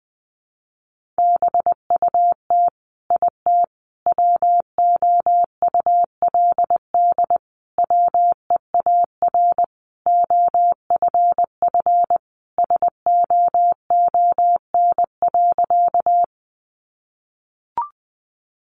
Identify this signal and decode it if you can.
Morse code, heard as a keyed tone, about 20 wpm, 700 Hz. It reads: BUT IT WOULD WEAR OFF SOON.
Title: Sentences from Top 1500 Words 20wpm